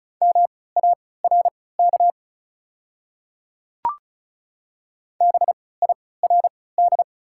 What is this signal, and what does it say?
Morse code, heard as a keyed tone, about 36 wpm, spaced as for 20 wpm, 690 Hz. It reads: MARK BIRD
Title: 3x Character Spacing - Top 500 Words - 35wpm